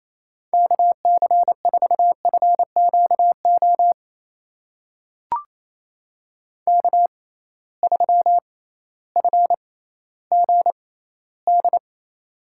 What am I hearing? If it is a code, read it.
Morse code, heard as a keyed tone, about 28 wpm, spaced as for 15 wpm, 700 Hz. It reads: KC4FQO K3FGD
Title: USA Call Signs - 6x Character Spacing - 28wpm